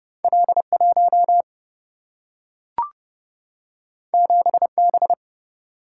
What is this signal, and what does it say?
Morse code, heard as a keyed tone, about 31 wpm, 700 Hz. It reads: L 1 7 B
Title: Single-Letter Single-Number 30wpm